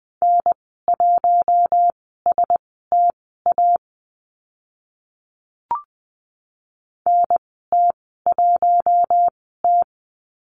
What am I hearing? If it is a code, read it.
Morse code, heard as a keyed tone, about 20 wpm, spaced as for 14 wpm, 700 Hz. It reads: N1STA NT1T
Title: 042 Lesson 8g - Number 1 - Call Signs 20wpm